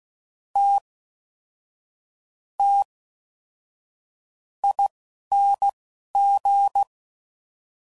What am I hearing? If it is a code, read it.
Morse code, heard as a keyed tone, about 16 wpm, spaced as for 6 wpm, 800 Hz. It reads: TTING